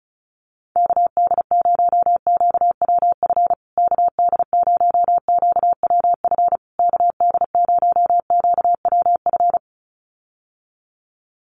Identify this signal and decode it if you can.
Morse code, heard as a keyed tone, about 34 wpm, 700 Hz. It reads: KD0QWF KD0QWF KD0QWF
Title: USA Call Signs Repeated 3x 35wpm